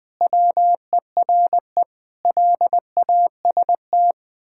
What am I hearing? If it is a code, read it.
Morse code, heard as a keyed tone, about 20 wpm, 700 Hz. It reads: WERE LAST
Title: Pangram Challenge v2 - 20wpm